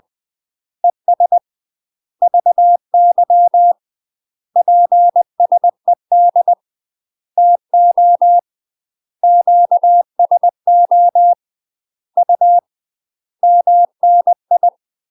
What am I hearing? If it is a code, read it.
Morse code, heard as a keyed tone, about 20 wpm, 700 Hz. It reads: ES VY PSED TO QSO U MNI
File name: UK MASTER QSOs TRAINING / 20WPM / UK MASTER QSOs 0067